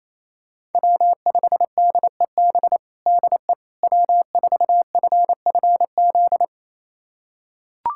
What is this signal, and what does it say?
Morse code, heard as a keyed tone, about 28 wpm, 700 Hz. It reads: W5DEB DE W4FFZ
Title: USA Callsign DE Callsign - 5 Characters - 28wpm